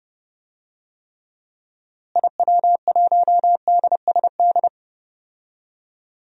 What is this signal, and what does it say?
Morse code, heard as a keyed tone, about 30 wpm, 700 Hz. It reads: IW1DSD
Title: DXCC-100 Call Signs - 6 Characters - Repeated 2x - 30wpm